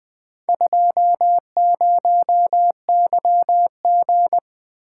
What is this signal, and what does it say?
Morse code, heard as a keyed tone, about 20 wpm, 700 Hz. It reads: 20YG
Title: Four Letter-Number Combinations - 20wpm